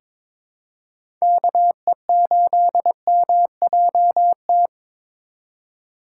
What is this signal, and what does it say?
Morse code, heard as a keyed tone, about 22 wpm, 700 Hz. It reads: KE8MJT
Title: USA Call Signs - 2x Character Spacing - 22wpm